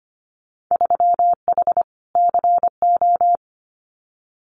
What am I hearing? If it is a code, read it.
Morse code, heard as a keyed tone, about 25 wpm, 700 Hz. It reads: 3H CO
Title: Winter Field Day Exchanges 25wpm